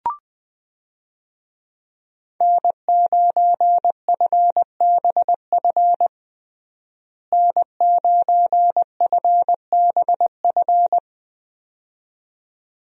Morse code, heard as a keyed tone, about 21 wpm, 700 Hz. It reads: N9FBF N9FBF
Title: USA Call Signs - 5 Characters - Repeated 2x - 20wpm